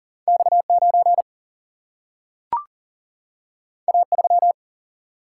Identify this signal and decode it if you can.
Morse code, heard as a keyed tone, about 40 wpm, 700 Hz. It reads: X 9 A 3
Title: Single-Letter Single-Number 40wpm